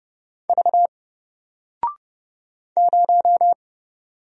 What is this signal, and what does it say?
Morse code, heard as a keyed tone, about 30 wpm, 700 Hz. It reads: V 0
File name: Single-Letter-Number-30wpm